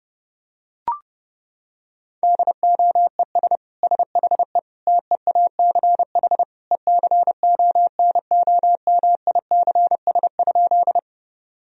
Morse code, heard as a keyed tone, about 30 wpm, 700 Hz. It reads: DOES SHE TEACH ECONOMICS?
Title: Sentences from Top 1000 Words - Review Part 3 - 30wpm